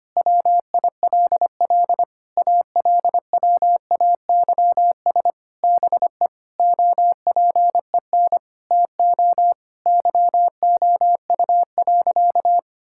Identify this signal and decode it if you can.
Morse code, heard as a keyed tone, about 25 wpm, 700 Hz. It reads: WILL ALWAYS BE OPEN TO YOU.
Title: Sentences from Top 1600 Words 25wpm